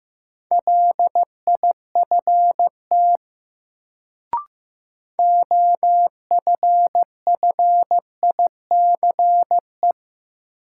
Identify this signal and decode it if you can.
Morse code, heard as a keyed tone, about 15 wpm, 700 Hz. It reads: LIFT OFFICE
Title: Sentences from Top 1000 Words - New Word Drill - 15wpm